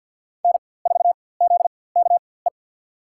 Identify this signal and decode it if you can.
Morse code, heard as a keyed tone, about 52 wpm, spaced as for 23 wpm, 700 Hz. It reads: N4ZKE
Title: USA Call Signs - 5 Characters - 4x Character Spacing - 50wpm